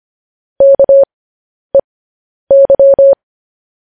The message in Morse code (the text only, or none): KEY